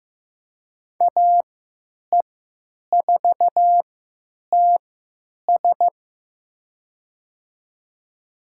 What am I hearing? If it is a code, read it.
Morse code, heard as a keyed tone, about 15 wpm, spaced as for 8 wpm, 700 Hz. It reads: AE4TS
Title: USA Call Signs - 3x Character Spacing - 15wpm